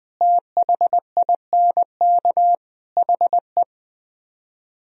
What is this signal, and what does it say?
Morse code, heard as a keyed tone, about 20 wpm, 700 Hz. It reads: THINK HE